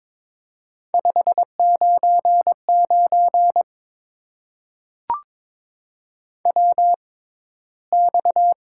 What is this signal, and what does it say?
Morse code, heard as a keyed tone, about 22 wpm, spaced as for 14 wpm, 700 Hz. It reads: 599 WX
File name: Top 90 QSO Elements - 6x Character Spacing - 22wpm